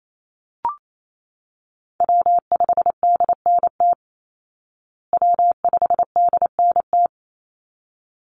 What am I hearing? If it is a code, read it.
Morse code, heard as a keyed tone, about 28 wpm, 700 Hz. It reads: W5DNT W5DNT
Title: Call Signs Repeated 3x - 28wpm